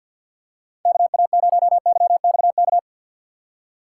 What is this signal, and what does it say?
Morse code, heard as a keyed tone, about 47 wpm, 700 Hz. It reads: KA0YXK
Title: USA Call Signs - 3x Character Spacing - 50wpm